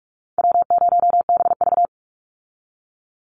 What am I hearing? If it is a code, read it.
Morse code, heard as a keyed tone, about 44 wpm, 700 Hz. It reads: W0BV
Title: Easy Call Signs Repeated 2x - 45wpm